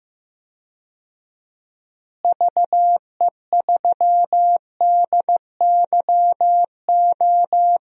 Morse code, heard as a keyed tone, about 15 wpm, 700 Hz. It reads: VE3DYO
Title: DXCC-100 Call Signs - 6 Characters - Repeated 3x - 15wpm